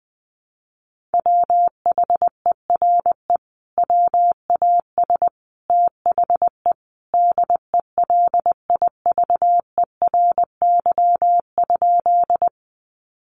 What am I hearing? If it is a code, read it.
Morse code, heard as a keyed tone, about 20 wpm, 700 Hz. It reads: WHERE WAS THE DELIVERY?
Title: Sentences from Top 1500 Words 20wpm